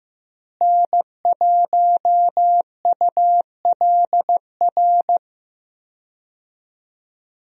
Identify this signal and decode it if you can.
Morse code, heard as a keyed tone, about 15 wpm, 700 Hz. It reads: N1ULR
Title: USA Call Signs - 5 Characters - Repeated 2x - 15wpm